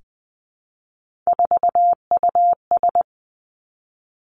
Morse code, heard as a keyed tone, about 20 wpm, 700 Hz. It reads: 4US
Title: Three Challenging Letter-Number Combinations 20wpm